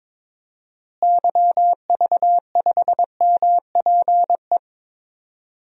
Morse code, heard as a keyed tone, about 21 wpm, 700 Hz. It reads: YV5MPE
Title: DXCC-100 Call Signs - 6 Characters - Repeated 3x - 22wpm